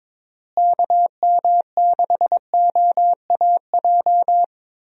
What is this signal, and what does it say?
Morse code, heard as a keyed tone, about 22 wpm, 700 Hz. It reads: KM6OAJ
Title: USA Call Signs - 6 Characters - Repeated 3x - 22wpm